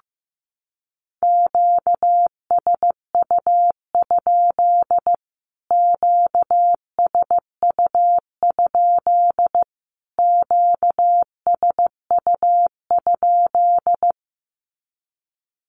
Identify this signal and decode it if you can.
Morse code, heard as a keyed tone, about 15 wpm, 700 Hz. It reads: QSU? QSU? QSU?
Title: Q-Codes Repeated 3x 15wpm